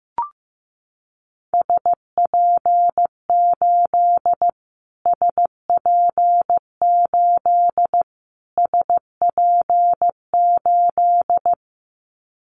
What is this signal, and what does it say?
Morse code, heard as a keyed tone, about 15 wpm, 700 Hz. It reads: SP8 SP8 SP8
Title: Callsign Prefixes for Top 25 DXCC - Repeated 3x - 15wpm